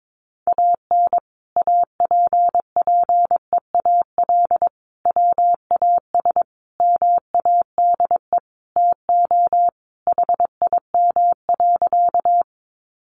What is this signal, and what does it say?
Morse code, heard as a keyed tone, about 22 wpm, 700 Hz. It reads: AN APPEAL WAS MADE TO HIM.
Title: Sentences from Top 1300 Words 22wpm